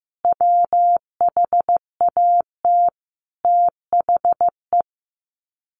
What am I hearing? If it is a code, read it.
Morse code, heard as a keyed tone, about 15 wpm, 700 Hz. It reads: WHAT THE
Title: Sets of 5 Words 15wpm